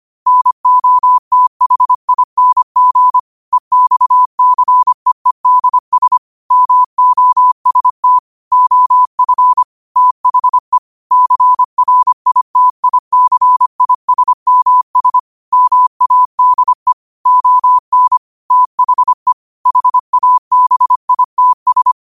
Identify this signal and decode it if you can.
Morse code, heard as a keyed tone, about 25 wpm, 1000 Hz. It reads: NOTHING EXCEEDS MOST OF THE CRITICISMS MADE ON THE HABITS